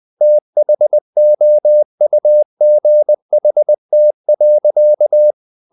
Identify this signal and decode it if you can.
Morse code, heard as a keyed tone, about 20 wpm, 600 Hz. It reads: THOUGHT.
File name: the old man and the sea part 2 20wpm